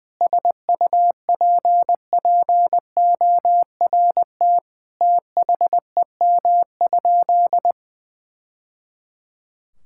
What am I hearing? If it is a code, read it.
Morse code, heard as a keyed tone, about 20 wpm, 700 Hz. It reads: SUPPORT THEM?
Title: Sentences from Top 1000 Words - Review Part 5 - 20wpm